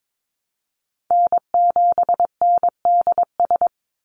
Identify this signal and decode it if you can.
Morse code, heard as a keyed tone, about 22 wpm, 700 Hz. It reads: N7NDS
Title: USA Call Signs 22wpm